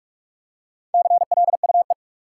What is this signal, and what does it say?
Morse code, heard as a keyed tone, about 42 wpm, 700 Hz. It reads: CLUE
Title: Four Letter Words - 45wpm